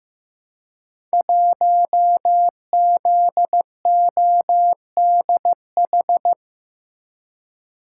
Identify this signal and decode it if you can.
Morse code, heard as a keyed tone, about 15 wpm, 700 Hz. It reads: 1ZODH
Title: Code Groups - 5 Letter:Numbers - 15wpm